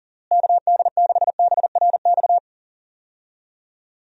Morse code, heard as a keyed tone, about 40 wpm, 700 Hz. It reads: KD6BRX
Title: USA Call Signs - 6 Characters - Repeated 3x - 40wpm